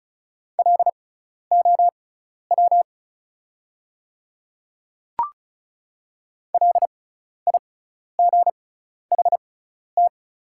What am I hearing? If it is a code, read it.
Morse code, heard as a keyed tone, about 36 wpm, spaced as for 12 wpm, 700 Hz. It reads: LOW LIGHT
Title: 6x Character Spacing - Top 200 Words - 35wpm